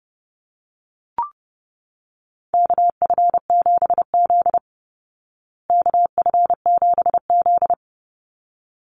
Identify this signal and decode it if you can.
Morse code, heard as a keyed tone, about 30 wpm, 700 Hz. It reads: KF7Z KF7Z
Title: Call Signs Repeated 3x - 30wpm